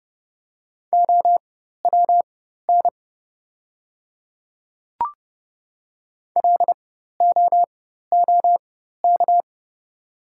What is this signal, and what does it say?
Morse code, heard as a keyed tone, about 30 wpm, spaced as for 14 wpm, 700 Hz. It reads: OWN LOOK
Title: 4x Character Spacing - Top 300 Words - 30wpm